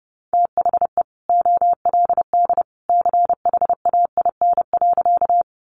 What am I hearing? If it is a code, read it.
Morse code, heard as a keyed tone, about 28 wpm, 700 Hz. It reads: THE OLD CHAIN.
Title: Sentences from Top 1500 Words 30wpm